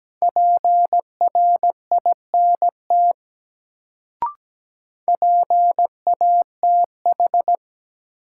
Morse code, heard as a keyed tone, about 17 wpm, 700 Hz. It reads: PRINT PATH
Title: Sentences from Top 1400 Words - New Word Drill Alt - 17wpm